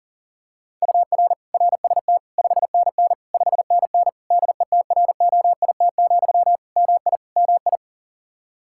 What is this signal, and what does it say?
Morse code, heard as a keyed tone, about 37 wpm, 700 Hz. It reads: UR RST 5NN 5NN DETROIT, MI MI
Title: Practice QSO - Exchange 1 - Beginning - 40wpm